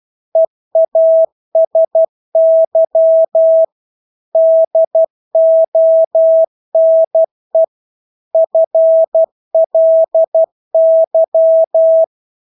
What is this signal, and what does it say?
Morse code, heard as a keyed tone, about 12 wpm, 650 Hz. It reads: EASY DONE FLY
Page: Engelske ord 12 wpm